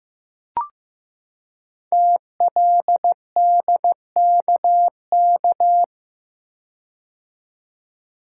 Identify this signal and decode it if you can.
Morse code, heard as a keyed tone, about 15 wpm, 700 Hz. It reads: TLDKK
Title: Code Groups - 5 Letters - 15wpm